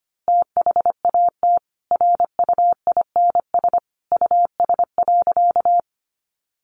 Morse code, heard as a keyed tone, about 25 wpm, 700 Hz. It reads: THAT RUINS US.
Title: A Sound Of Thunder 25wpm